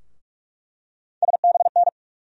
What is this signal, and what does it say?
Morse code, heard as a keyed tone, about 42 wpm, 700 Hz. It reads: SBN
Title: USA Call Sign Postfixes - 3 Letters 45wpm